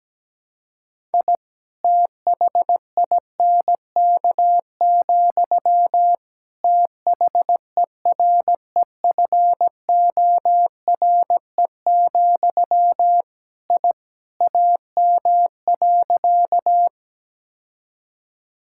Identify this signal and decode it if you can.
Morse code, heard as a keyed tone, about 17 wpm, 700 Hz. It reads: I THINK, THEREFORE, I AM.
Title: Sentences from Top 1000 Words - Review Part 5 - 17wpm